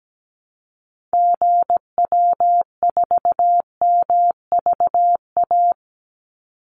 Morse code, heard as a keyed tone, about 17 wpm, 700 Hz. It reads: GW4MVA
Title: Call Signs - 17wpm